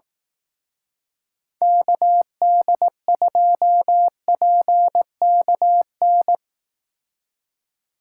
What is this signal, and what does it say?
Morse code, heard as a keyed tone, about 18 wpm, 700 Hz. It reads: KD2PKN
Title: Speed Racing - USA 2x3 Call Signs 15wpm